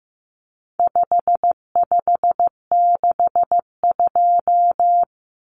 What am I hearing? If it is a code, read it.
Morse code, heard as a keyed tone, about 15 wpm, 700 Hz. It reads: 5562
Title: Common Cut Numbers - 4-Digit 15wpm